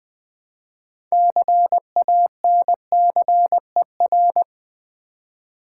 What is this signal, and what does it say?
Morse code, heard as a keyed tone, about 20 wpm, 700 Hz. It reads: CANCER
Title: 6 Letter Words - First 3 Letters Given - 20wpm